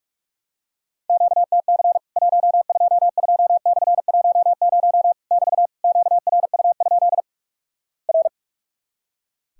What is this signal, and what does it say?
Morse code, heard as a keyed tone, about 46 wpm, 700 Hz. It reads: QTC 122/10 <BT> QRU? R
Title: Worked All Europe DX Contest - QTC Exchange - 45wpm